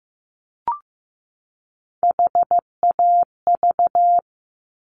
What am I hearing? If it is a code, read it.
Morse code, heard as a keyed tone, about 15 wpm, 700 Hz. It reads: HAV
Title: Top Three Letter Combinations in Words 15wpm